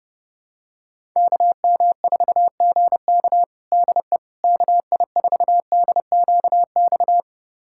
Morse code, heard as a keyed tone, about 32 wpm, 700 Hz. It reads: KM4GK DE KI4DQX
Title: USA Callsign DE Callsign 30wpm